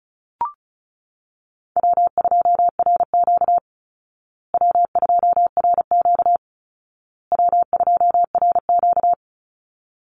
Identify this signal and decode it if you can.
Morse code, heard as a keyed tone, about 34 wpm, 700 Hz. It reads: W2RQ W2RQ W2RQ
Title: Call Signs Repeated 2x - 35wpm